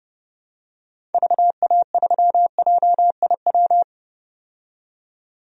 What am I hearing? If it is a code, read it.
Morse code, heard as a keyed tone, about 30 wpm, 700 Hz. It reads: VA3JIW
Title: DXCC-100 Call Signs - 6 Characters - 30wpm